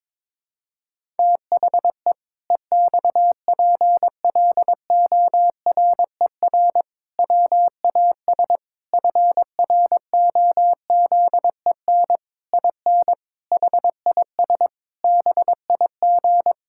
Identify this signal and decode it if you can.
Morse code, heard as a keyed tone, about 22 wpm, 700 Hz. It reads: THE EXPLORER WAS FROZEN IN HIS BIG
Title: Pangram Challenge v2 - 22wpm